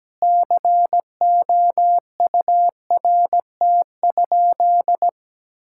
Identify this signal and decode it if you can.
Morse code, heard as a keyed tone, about 17 wpm, 700 Hz. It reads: COURT?
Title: Sentences from Top 1600 Words 17wpm